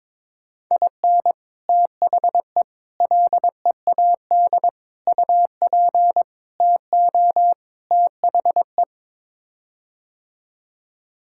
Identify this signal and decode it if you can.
Morse code, heard as a keyed tone, about 22 wpm, 700 Hz. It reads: IN THE LEAD UP TO THE
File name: practice / 22wpm / Sets of 6 Words - 22wpm